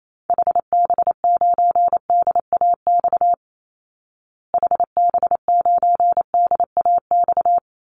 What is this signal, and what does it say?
Morse code, heard as a keyed tone, about 28 wpm, 700 Hz. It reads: HB9DAX HB9DAX
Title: Call Signs Repeated 2x - 28wpm